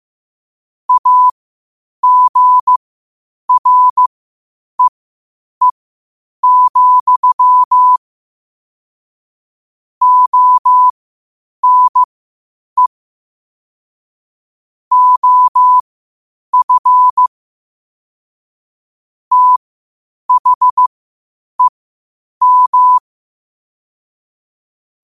Morse code, heard as a keyed tone, about 15 wpm, spaced as for 8 wpm, 1000 Hz. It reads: AGREE, ONE OF THEM
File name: qotdfiles / 8wpm